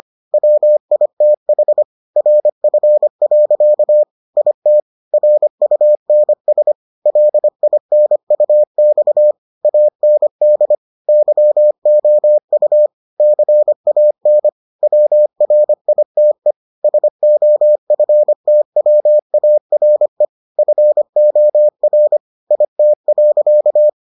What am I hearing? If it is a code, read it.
Morse code, heard as a keyed tone, about 25 wpm, 600 Hz. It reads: WITH RF. IT RUNS LINUX AND YOU CAN WRITE SOFTWARE FOR IT.